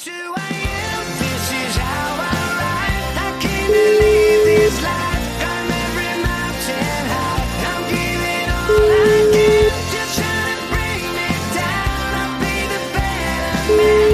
Ringback tone (Europe, 425 Hz), with noise.